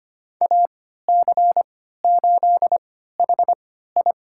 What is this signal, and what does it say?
Morse code, heard as a keyed tone, about 25 wpm, spaced as for 14 wpm, 700 Hz. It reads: AC8HI
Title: USA Call Signs - 5 Characters - 3x Character Spacing - 25wpm